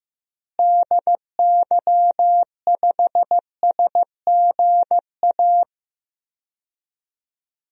Morse code, heard as a keyed tone, about 15 wpm, 700 Hz. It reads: DY5SGA
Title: DXCC-100 Call Signs - 6 Characters - Repeated 2x - 15wpm